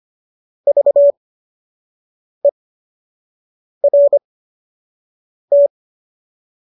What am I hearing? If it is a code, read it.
Morse code, heard as a keyed tone, about 25 wpm, spaced as for 6 wpm, 570 Hz. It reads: VERT